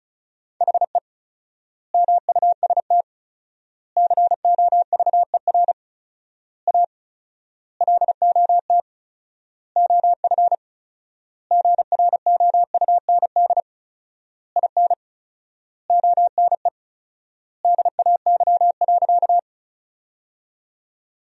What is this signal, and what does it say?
Morse code, heard as a keyed tone, about 34 wpm, 700 Hz. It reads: HE MUST COVER A LOT OF GROUND IN ONE DAY.
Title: 4x Word-Spacing - Sentences from Top 500 Words - 35wpm